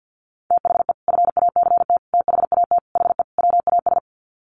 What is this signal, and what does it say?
Morse code, heard as a keyed tone, about 50 wpm, 700 Hz. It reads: THE FACT THAT HE WAS
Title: Sets of 5 Words 50wpm